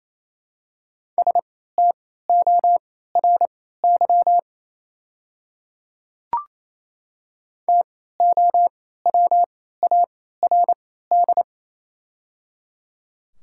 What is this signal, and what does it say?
Morse code, heard as a keyed tone, about 28 wpm, spaced as for 16 wpm, 700 Hz. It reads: STORY TOWARD
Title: 3x Character Spacing - Top 500 Words - 28wpm